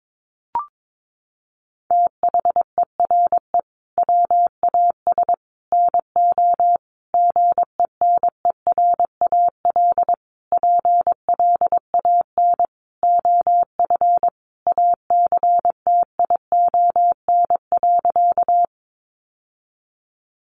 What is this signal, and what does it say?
Morse code, heard as a keyed tone, about 22 wpm, 700 Hz. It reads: THERE WAS NO GENERAL PLAN OF ACTION.
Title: Sentences from Top 700 Words 22wpm